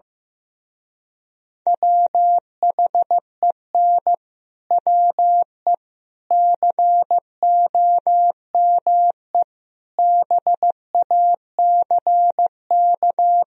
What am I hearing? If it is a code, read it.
Morse code, heard as a keyed tone, about 15 wpm, 700 Hz. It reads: WHEN WE COME BACK
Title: Speed-Racing - Sets of 4 Words 15wpm